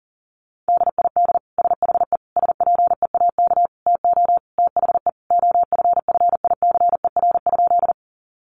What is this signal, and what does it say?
Morse code, heard as a keyed tone, about 40 wpm, 700 Hz. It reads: DID SHE SPEAK TO THE OFFICER?
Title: Sentences from Top 1200 Words 40wpm